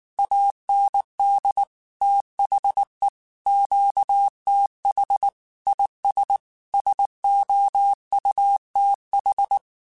Morse code, heard as a keyed tone, about 19 wpm, 800 Hz. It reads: AND THE QTH IS SOUTH